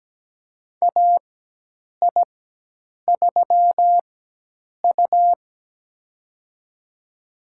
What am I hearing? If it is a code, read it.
Morse code, heard as a keyed tone, about 17 wpm, spaced as for 8 wpm, 700 Hz. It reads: AI3U
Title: USA Call Signs - 4 Characters - 4x Character Spacing - 17wpm